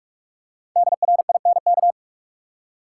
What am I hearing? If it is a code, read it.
Morse code, heard as a keyed tone, about 43 wpm, 700 Hz. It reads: DRINK
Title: Five Letter Words - 45wpm